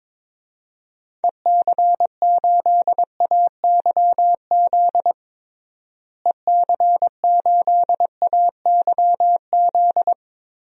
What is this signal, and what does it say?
Morse code, heard as a keyed tone, about 22 wpm, 700 Hz. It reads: EC8AYZ EC8AYZ
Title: DXCC-100 Call Signs - 6 Characters - Repeated 3x - 22wpm